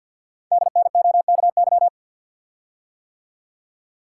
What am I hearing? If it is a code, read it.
Morse code, heard as a keyed tone, about 51 wpm, 690 Hz. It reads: DNOKY
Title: Code Groups - 5 Letters - 50wpm